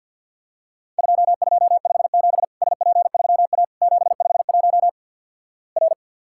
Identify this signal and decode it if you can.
Morse code, heard as a keyed tone, about 50 wpm, 700 Hz. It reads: 2257 SP3A 751 R
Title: Worked All Europe DX Contest - QTC Exchange with Repeats - 50wpm